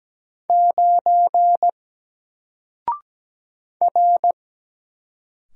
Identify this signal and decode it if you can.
Morse code, heard as a keyed tone, about 17 wpm, 700 Hz. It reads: 9 R
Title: Single Character (Letter:Number:Punctuation:Prosign) 17wpm